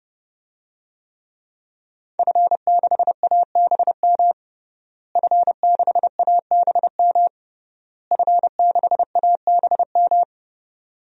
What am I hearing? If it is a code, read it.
Morse code, heard as a keyed tone, about 30 wpm, 700 Hz. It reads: F6ABM F6ABM F6ABM
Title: DXCC-100 Call Signs - 5 Characters - Repeated 3x - 30wpm